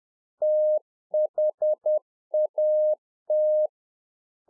Morse code, heard as a keyed tone, about 10 wpm, 610 Hz. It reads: THAT
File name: 2011-04-mp3 / UNFOGETTABLE QSO 10WPM